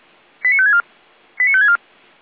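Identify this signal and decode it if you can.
Five-tone selective call (ZVEI), heard as quick runs of five tones, tones of about 70 ms. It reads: 89564 89564
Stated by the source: ZVEI five-tone selective call 89338 received via SDR.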